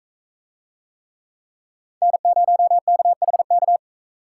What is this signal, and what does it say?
Morse code, heard as a keyed tone, about 45 wpm, 700 Hz. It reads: N0KHK